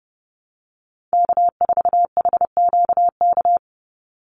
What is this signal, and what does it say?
Morse code, heard as a keyed tone, about 28 wpm, 700 Hz. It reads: K4HQK
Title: Call Signs - 30wpm